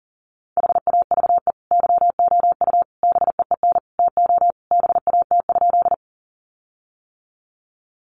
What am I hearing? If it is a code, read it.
Morse code, heard as a keyed tone, about 40 wpm, 700 Hz. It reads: HAVE YOU BEEN TO BAT?
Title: Sentences from Top 1500 Words 40wpm